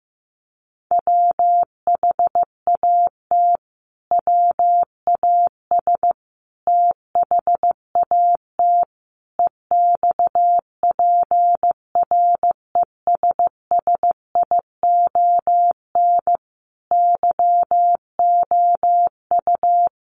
Morse code, heard as a keyed tone, about 15 wpm, 700 Hz. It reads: WHAT WAS THAT EXPRESSION YOU
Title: Sentences from Top 1200 Words 15wpm